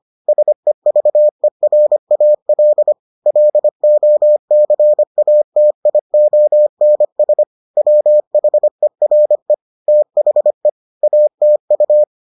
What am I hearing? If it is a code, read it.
Morse code, heard as a keyed tone, about 25 wpm, 600 Hz. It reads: SEVERAL LOCATIONS WHERE THE ATU